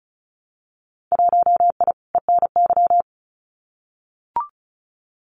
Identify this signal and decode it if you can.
Morse code, heard as a keyed tone, about 34 wpm, 700 Hz. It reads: 1I ENY
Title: Winter Field Day Exchanges Encore 35wpm